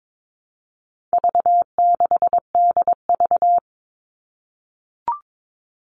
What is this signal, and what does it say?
Morse code, heard as a keyed tone, about 22 wpm, 700 Hz. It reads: V6DV
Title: All Cut Numbers - 4-Digit 22wpm